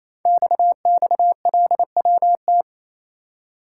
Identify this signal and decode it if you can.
Morse code, heard as a keyed tone, about 29 wpm, 700 Hz. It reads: XXLWT
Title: Code Groups - 5 Letters - 28wpm